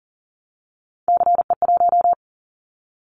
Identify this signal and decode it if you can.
Morse code, heard as a keyed tone, about 40 wpm, 690 Hz. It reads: CE1
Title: Callsign Prefixes for Top 100 DXCC - 40wpm